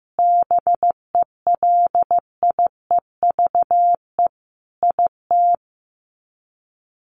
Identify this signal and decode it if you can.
Morse code, heard as a keyed tone, about 15 wpm, 700 Hz. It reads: BELIEVE IT
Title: Sets of 5 Words Encore 15wpm